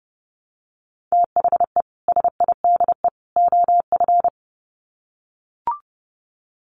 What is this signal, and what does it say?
Morse code, heard as a keyed tone, about 30 wpm, 700 Hz. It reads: THE SIDE OF
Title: Sets of 3 Words 30wpm